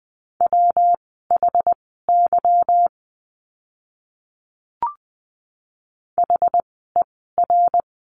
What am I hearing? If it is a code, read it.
Morse code, heard as a keyed tone, about 20 wpm, spaced as for 14 wpm, 700 Hz. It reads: WHY HER
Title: 192 Lesson 31f - Letter v - Words 20wpm